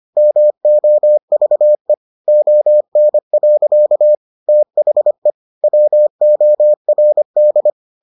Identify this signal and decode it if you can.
Morse code, heard as a keyed tone, about 25 wpm, 600 Hz. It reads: MOVE ON. THE WORD